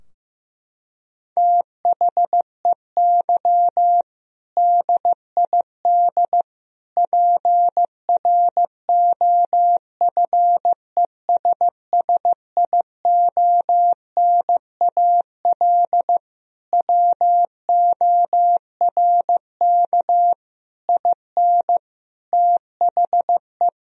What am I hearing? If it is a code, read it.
Morse code, heard as a keyed tone, about 15 wpm, 700 Hz. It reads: THEY DID PROFESSIONAL WORK IN THE
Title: Sentences from Top 1000 Words - Review Part 4 - 15wpm